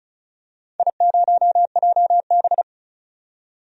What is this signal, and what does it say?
Morse code, heard as a keyed tone, about 36 wpm, 700 Hz. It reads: I0JB
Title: DXCC-100 Call Signs - 4 Characters - 35wpm